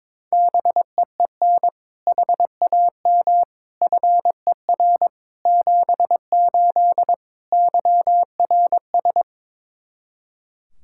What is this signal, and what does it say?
Morse code, heard as a keyed tone, about 22 wpm, 700 Hz. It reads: BEEN HAM FER 78 YRS
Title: K1USN SST QSO Practice 22wpm